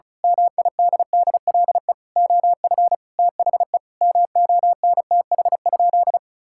Morse code, heard as a keyed tone, about 36 wpm, 700 Hz. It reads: MIDDLE OF THE MONTH?
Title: Sentences from Top 1000 Words - Review Part 6 - 35wpm